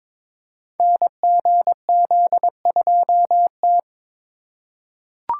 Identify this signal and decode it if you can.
Morse code, heard as a keyed tone, about 22 wpm, 700 Hz. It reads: NGZ2T
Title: Code Groups - 5 Letter:Numbers - 22wpm